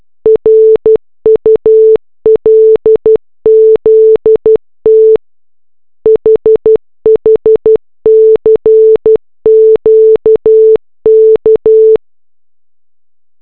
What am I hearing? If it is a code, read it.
Morse code, heard as a keyed tone, about 12 wpm, 440 Hz. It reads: RULZT HHCQK